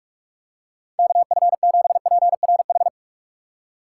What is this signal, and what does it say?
Morse code, heard as a keyed tone, about 43 wpm, 700 Hz. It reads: KF7PRH
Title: USA Call Signs - 4x Character Spacing - 45wpm